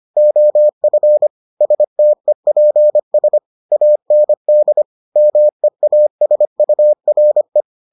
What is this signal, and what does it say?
Morse code, heard as a keyed tone, about 25 wpm, 600 Hz. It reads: OF STEPS AND MEASURE